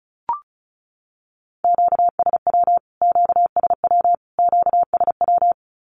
Morse code, heard as a keyed tone, about 34 wpm, 700 Hz. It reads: QSW QSW QSW
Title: Q-Codes Repeated 3x 35wpm